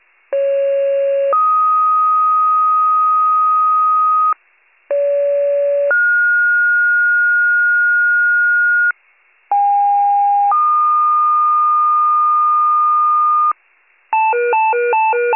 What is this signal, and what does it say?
Two-tone page, heard as a one-second tone, then a three-second tone, about 550 then 1250 Hz, about 570 then 1430 Hz, about 810 then 1220 Hz. It is heard from 0.3 to 13.5 s.